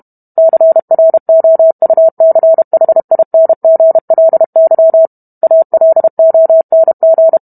Morse code, heard as a keyed tone, about 32 wpm, 650 Hz. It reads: CROUCHINGLY ALONG